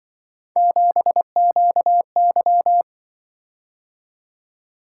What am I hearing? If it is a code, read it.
Morse code, heard as a keyed tone, about 24 wpm, 700 Hz. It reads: 7QY
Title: Speed-Racing - Three Letter-Number Combinations - 20wpm